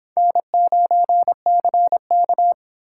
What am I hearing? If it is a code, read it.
Morse code, heard as a keyed tone, about 23 wpm, 700 Hz. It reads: N9CK